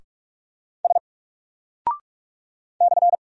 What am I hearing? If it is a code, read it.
Morse code, heard as a keyed tone, about 44 wpm, 700 Hz. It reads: S /
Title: Single Character (Letter:Number:Punctuation:Prosign) 45wpm